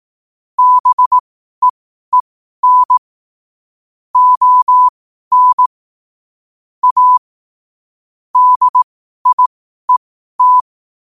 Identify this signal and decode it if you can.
Morse code, heard as a keyed tone, about 18 wpm, spaced as for 12 wpm, 1000 Hz. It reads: BEEN ON A DIET
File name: 12wpm